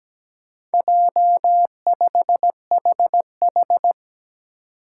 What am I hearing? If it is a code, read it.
Morse code, heard as a keyed tone, about 17 wpm, 700 Hz. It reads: J5HH
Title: Four Challenging Letter-Number Combinations 17wpm